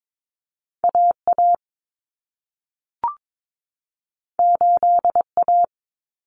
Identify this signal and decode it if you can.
Morse code, heard as a keyed tone, about 22 wpm, 700 Hz. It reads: AA 8A
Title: Common Cut Numbers - 2-Digit 22wpm